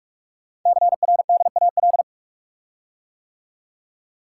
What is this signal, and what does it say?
Morse code, heard as a keyed tone, about 44 wpm, 700 Hz. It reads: CRDAL
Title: Code Groups - 5 Letters - 45wpm